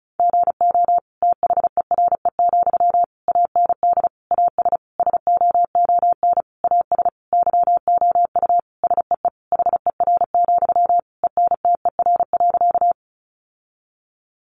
Morse code, heard as a keyed tone, about 34 wpm, 700 Hz. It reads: GO THERE, AND AS SOON AS YOU SEE HER, ENTER.
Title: Sentences from Top 1300 Words 35wpm